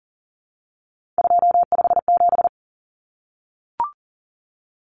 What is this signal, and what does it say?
Morse code, heard as a keyed tone, about 40 wpm, 700 Hz. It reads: 257
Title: Common Cut Numbers - 3-Digit 40wpm